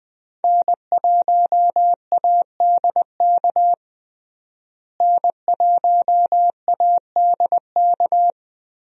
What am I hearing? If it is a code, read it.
Morse code, heard as a keyed tone, about 20 wpm, 700 Hz. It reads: N1ADK N1ADK
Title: USA Call Signs - 5 Characters - Repeated 3x - 20wpm